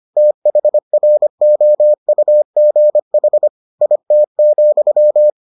Morse code, heard as a keyed tone, about 25 wpm, 600 Hz. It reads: THROUGH IT,